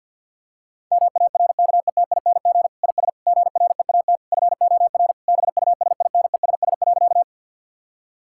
Morse code, heard as a keyed tone, about 50 wpm, 700 Hz. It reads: MARKETING IS GREAT FOR BUSINESS.
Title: Sentences from Top 1000 Words - Review Part 5 - 50wpm